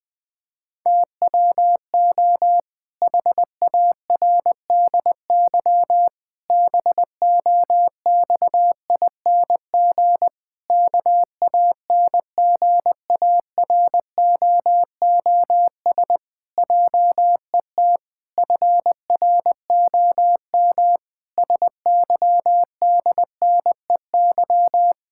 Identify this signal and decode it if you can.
Morse code, heard as a keyed tone, about 20 wpm, 700 Hz. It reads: TWO HARDY BOXING KANGAROOS JET FROM SYDNEY
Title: Pangram Challenge v2 - 20wpm